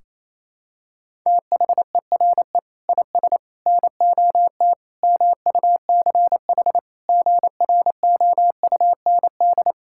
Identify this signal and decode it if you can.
Morse code, heard as a keyed tone, about 27 wpm, 700 Hz. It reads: THERE IS NOT MUCH GROUND
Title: Sentences from Top 1600 Words 28wpm